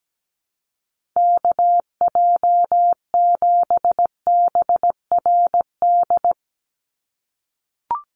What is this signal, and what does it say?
Morse code, heard as a keyed tone, about 17 wpm, 700 Hz. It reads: KJ7BRD
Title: USA Call Signs 17wpm